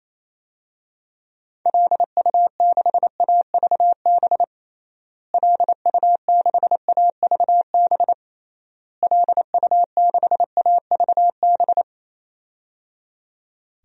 Morse code, heard as a keyed tone, about 28 wpm, 700 Hz. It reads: LU6AVB LU6AVB LU6AVB
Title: DXCC-100 Call Signs - 6 Characters - Repeated 3x - 28wpm